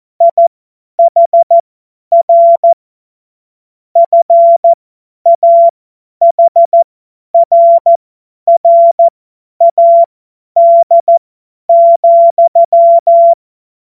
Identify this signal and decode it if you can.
Morse code, heard as a keyed tone, about 14 wpm, spaced as for 10 wpm, 680 Hz. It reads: IHR FAHRRAD,